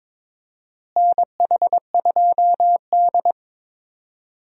Morse code, heard as a keyed tone, about 21 wpm, 700 Hz. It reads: NH2D
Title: USA Call Signs - 4 Characters - Repeated 3x - 22wpm